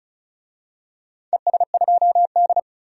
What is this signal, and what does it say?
Morse code, heard as a keyed tone, about 36 wpm, 700 Hz. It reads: ES2D
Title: DXCC-100 Call Signs - 4 Characters - 35wpm